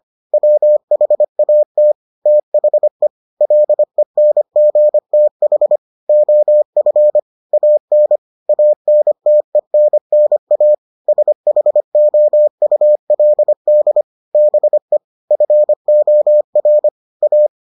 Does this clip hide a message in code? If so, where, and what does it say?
Yes: Morse code from 0.3 to 17.5 s, heard as a keyed tone, about 25 wpm, 600 Hz. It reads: WHAT THE LENGTH OF AN ANTENNA SHOULD BE FOR A